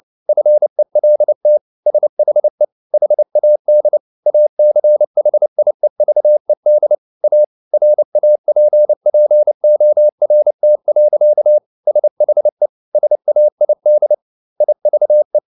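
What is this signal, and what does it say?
Morse code, heard as a keyed tone, about 29 wpm, 600 Hz. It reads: FELT SHE HAD ACHIEVED A RAPPORT. SHE SAID IVE
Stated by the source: Morse code at 29 wpm, sending Ken Follett's novel Never in English, from CW med Gnister.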